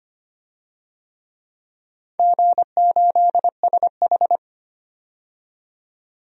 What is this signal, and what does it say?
Morse code, heard as a keyed tone, about 25 wpm, 700 Hz. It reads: G8SH
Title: DXCC-100 Call Signs - 4 Characters - 25wpm